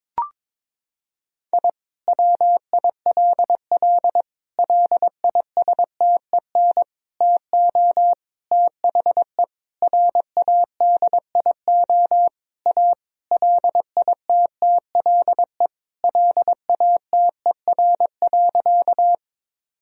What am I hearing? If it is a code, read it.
Morse code, heard as a keyed tone, about 22 wpm, 700 Hz. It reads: I WILL LISTEN TO THE RADIO A LITTLE LATER.
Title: Sentences from Top 1000 Words - Review Part 3 - 22wpm